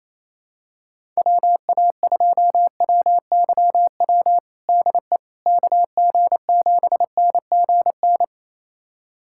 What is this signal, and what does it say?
Morse code, heard as a keyed tone, about 27 wpm, 700 Hz. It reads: WA2WYW DE KG7NGN
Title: USA Callsign DE Callsign - 6 Characters - 28wpm